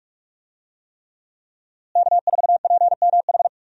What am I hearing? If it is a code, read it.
Morse code, heard as a keyed tone, about 46 wpm, 700 Hz. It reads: K4PMH
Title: USA Call Signs - 5 Characters - Repeated 3x - 45wpm